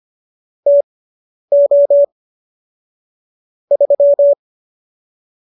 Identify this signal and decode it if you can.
Morse code, heard as a keyed tone, about 25 wpm, spaced as for 10 wpm, 570 Hz. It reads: TO 3
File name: qso219 10